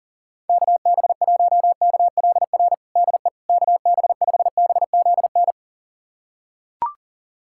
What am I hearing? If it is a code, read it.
Morse code, heard as a keyed tone, about 40 wpm, 700 Hz. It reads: KB1KLR DE KB5BZN
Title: USA Callsign DE Callsign - 6 Characters - 40wpm